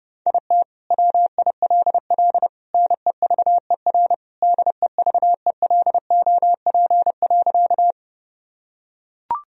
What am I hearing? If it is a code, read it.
Morse code, heard as a keyed tone, about 30 wpm, 700 Hz. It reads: IT WILL NEVER DEVELOP.
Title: Sentences from Top 1000 Words - Review Part 2 - 30wpm